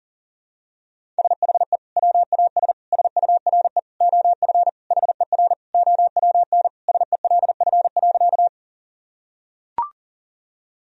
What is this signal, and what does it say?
Morse code, heard as a keyed tone, about 40 wpm, 700 Hz. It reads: SHE WAS SURE OF HER OWN SELF.
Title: Sentences from Top 1000 Words - Review Part 1 - 40wpm